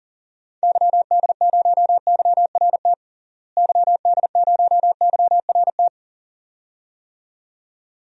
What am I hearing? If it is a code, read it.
Morse code, heard as a keyed tone, about 39 wpm, 700 Hz. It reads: YD0YRT YD0YRT
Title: DXCC-100 Call Signs - 6 Characters - Repeated 2x - 40wpm